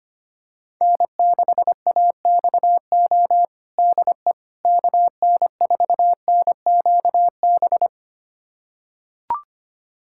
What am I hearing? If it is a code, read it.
Morse code, heard as a keyed tone, about 25 wpm, 700 Hz. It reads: N6AXO DE KN4NQB
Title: USA Callsign DE Callsign 25wpm